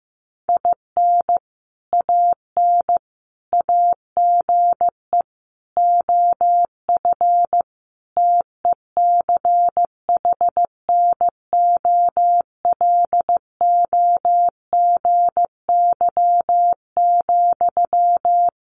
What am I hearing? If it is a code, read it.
Morse code, heard as a keyed tone, about 15 wpm, 700 Hz. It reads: IN AN AGE OF TECHNOLOGY,
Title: Sentences from Top 1200 Words 15wpm